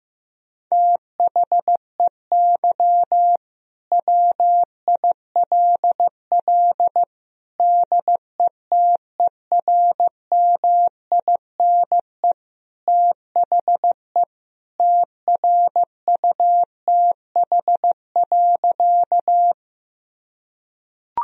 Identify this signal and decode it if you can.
Morse code, heard as a keyed tone, about 15 wpm, 700 Hz. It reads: THEY WILL DETERMINE THE TRUTH.
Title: Sentences from Top 1000 Words - Review Part 4 - 15wpm